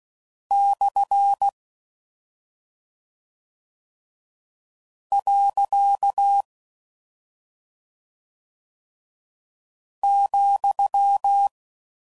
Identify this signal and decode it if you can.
Morse code, heard as a keyed tone, about 16 wpm, 800 Hz. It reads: / . ,